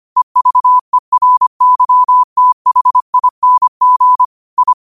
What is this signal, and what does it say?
Morse code, heard as a keyed tone, about 25 wpm, 1000 Hz. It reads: EVERYTHING I
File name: qotdfiles / 25wpm